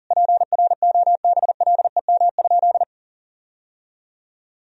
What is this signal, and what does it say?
Morse code, heard as a keyed tone, about 40 wpm, 700 Hz. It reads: PROBLEM?
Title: Sentences from Top 1000 Words - Review Part 4 - 40wpm